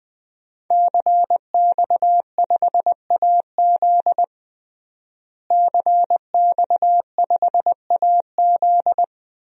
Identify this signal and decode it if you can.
Morse code, heard as a keyed tone, about 21 wpm, 700 Hz. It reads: CX5AZ CX5AZ
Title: DXCC-100 Call Signs - 5 Characters - Repeated 3x - 20wpm